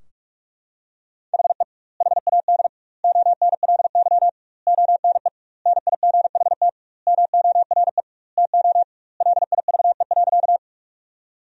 Morse code, heard as a keyed tone, about 46 wpm, 700 Hz. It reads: HE HAD ONLY ONE NIGHT MORE TO LIVE.
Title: 2x Word-Spacing - Sentences from Top 300 Words - 45wpm